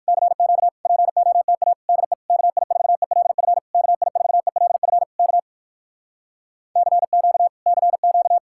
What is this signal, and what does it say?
Morse code, heard as a keyed tone, about 59 wpm, 700 Hz. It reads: CQ POTA DE KI4ELF KI4ELF K CQ CQ